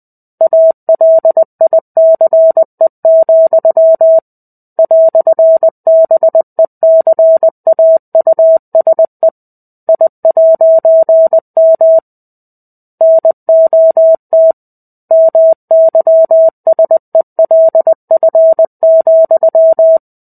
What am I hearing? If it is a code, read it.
Morse code, heard as a keyed tone, about 20 wpm, 650 Hz. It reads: ALICE, "BECAUSE I'M NOT MYSELF,
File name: 05-Chapter-05 20 WPM